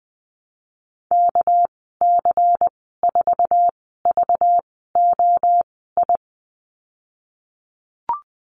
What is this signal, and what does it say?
Morse code, heard as a keyed tone, about 20 wpm, spaced as for 15 wpm, 700 Hz. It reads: KC4VOI
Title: 213 Lesson 34g - Number 8 - Call Signs 20wpm